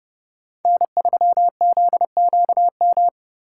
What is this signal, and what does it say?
Morse code, heard as a keyed tone, about 31 wpm, 700 Hz. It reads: N3ZQM